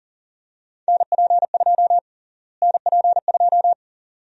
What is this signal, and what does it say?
Morse code, heard as a keyed tone, about 40 wpm, 700 Hz. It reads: NP2 NP2
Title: USA Call Sign Prefixes - Repeated 2x - 40wpm